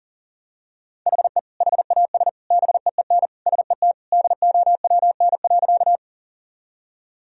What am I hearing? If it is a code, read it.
Morse code, heard as a keyed tone, about 40 wpm, 700 Hz. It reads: HE HAS BEEN SET DOWN.
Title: Sentences from Top 1000 Words - Review Part 1 - 40wpm